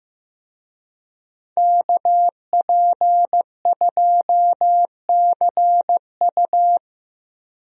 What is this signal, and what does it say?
Morse code, heard as a keyed tone, about 15 wpm, 700 Hz. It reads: KP2CU
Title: DXCC-100 Call Signs - 5 Characters - 15wpm